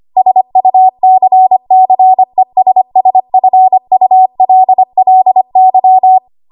Morse code, heard as a keyed tone, about 25 wpm, 750 Hz. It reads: SUCCESSFULLY
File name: Archive / 25 WPM / 25WPM